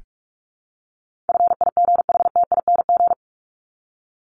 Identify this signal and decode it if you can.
Morse code, heard as a keyed tone, about 44 wpm, 700 Hz. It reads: FIGHTING
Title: ING Suffix - Top 100 Words with Base Words 45wpm